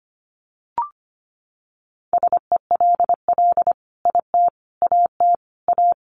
Morse code, heard as a keyed tone, about 25 wpm, 700 Hz. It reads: SELL IT AT A
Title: Sentences from Top 1200 Words 25wpm